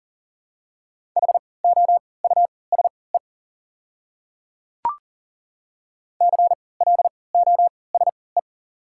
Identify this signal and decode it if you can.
Morse code, heard as a keyed tone, about 40 wpm, spaced as for 23 wpm, 690 Hz. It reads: HOUSE CLOSE
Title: 3x Character Spacing - Top 300 Words - 40wpm